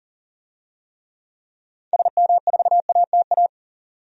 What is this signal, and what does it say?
Morse code, heard as a keyed tone, about 40 wpm, 700 Hz. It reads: SM4ATA
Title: DXCC-100 Call Signs - 6 Characters - 40wpm